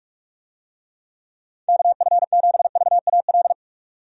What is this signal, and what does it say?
Morse code, heard as a keyed tone, about 46 wpm, 700 Hz. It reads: KF7VAL
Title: USA Call Signs - 6 Characters - 45wpm